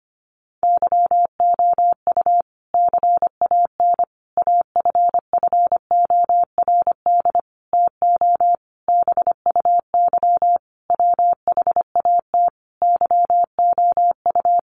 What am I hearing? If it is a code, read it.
Morse code, heard as a keyed tone, about 25 wpm, 700 Hz. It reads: YOU CAN AFFORD TO BUY WHAT YOU
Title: Sentences from Top 1100 Words 25wpm